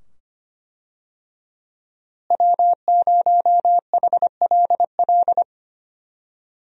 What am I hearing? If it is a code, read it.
Morse code, heard as a keyed tone, about 25 wpm, 700 Hz. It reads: W0HLL
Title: USA Call Signs - 5 Characters - Repeated 2x - 25wpm